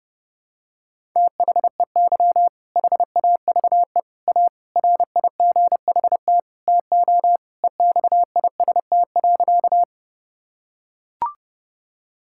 Morse code, heard as a keyed tone, about 31 wpm, 700 Hz. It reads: THEY HAVE A RIGHT TO EXIST.
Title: Sentences from Top 1000 Words - Review Part 5 - 30wpm